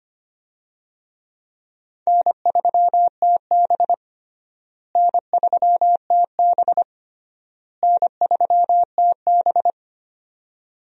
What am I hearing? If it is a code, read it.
Morse code, heard as a keyed tone, about 25 wpm, 700 Hz. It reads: N3TB N3TB N3TB
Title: USA Call Signs - 4 Characters - Repeated 3x - 25wpm